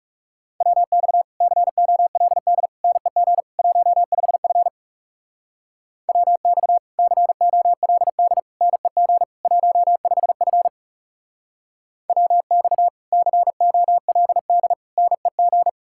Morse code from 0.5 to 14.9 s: WX COLD NEG 15F WX COLD NEG 15F WX COLD

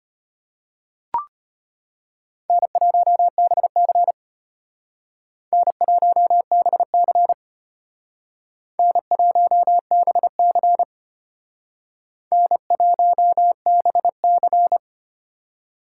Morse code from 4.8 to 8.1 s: N1BC

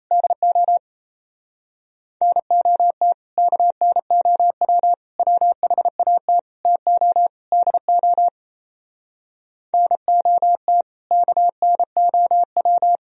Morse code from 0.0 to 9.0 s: DO NOT KNOW WHAT TO DO

